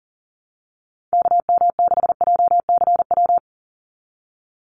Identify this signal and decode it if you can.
Morse code, heard as a keyed tone, about 40 wpm, 700 Hz. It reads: KM6JCW
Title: USA Call Signs 40wpm